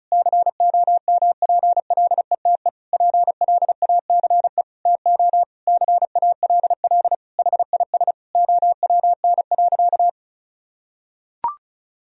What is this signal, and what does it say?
Morse code, heard as a keyed tone, about 34 wpm, 700 Hz. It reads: COMPLETE PLACE TO CALL HIS OWN.
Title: Sentences from Top 1000 Words - Review Part 2 - 35wpm